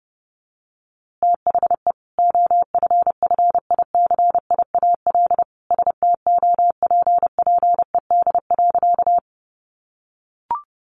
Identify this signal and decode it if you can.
Morse code, heard as a keyed tone, about 30 wpm, 700 Hz. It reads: THE OFFICIAL STOPPED.
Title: A Sound Of Thunder 30wpm